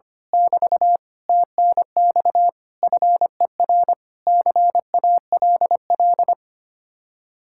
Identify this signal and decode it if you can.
Morse code, heard as a keyed tone, about 24 wpm, 700 Hz. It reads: <BT> TNX FER CALL
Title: Speed-Racing Common QSO Phrases - 25wpm